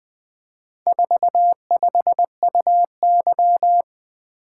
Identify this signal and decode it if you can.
Morse code, heard as a keyed tone, about 20 wpm, 700 Hz. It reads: 45UY
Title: Four Letter-Number Combinations - 20wpm